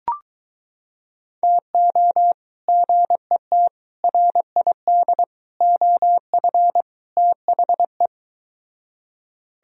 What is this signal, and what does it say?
Morse code, heard as a keyed tone, about 23 wpm, 700 Hz. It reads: TO GET RID OF THE